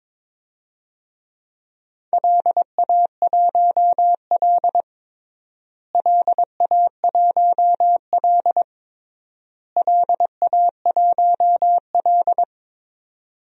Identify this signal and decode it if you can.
Morse code, heard as a keyed tone, about 22 wpm, 700 Hz. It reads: LA1L LA1L LA1L
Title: DXCC-100 Call Signs - 4 Characters - Repeated 3x - 22wpm